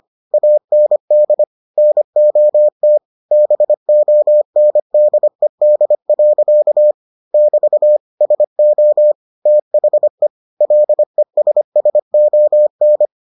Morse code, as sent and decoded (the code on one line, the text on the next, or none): .- -. -.. / -. --- - / -... --- -. -.. -.. .-.-.- / -...- / ... --- / - .... / .-.. ... ... --- -.
AND NOT BONDED. <BT> SO THE LESSON